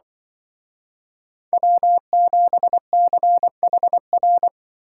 Morse code, heard as a keyed tone, about 24 wpm, 700 Hz. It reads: W7CHR
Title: Speed Racing - USA 1x3 Call Signs 20wpm